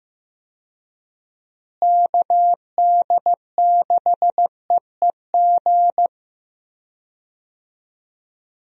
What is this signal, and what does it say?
Morse code, heard as a keyed tone, about 16 wpm, 700 Hz. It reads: KD6EEG
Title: USA Call Signs - 6 Characters - Repeated 2x - 15wpm